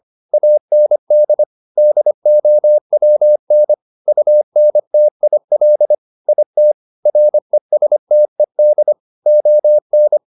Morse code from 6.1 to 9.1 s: IT RESTED

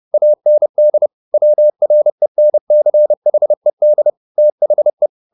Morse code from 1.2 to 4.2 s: WRENCHED

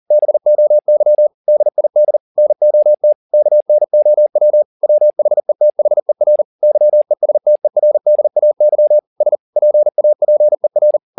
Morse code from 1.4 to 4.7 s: DID NOT KNOW

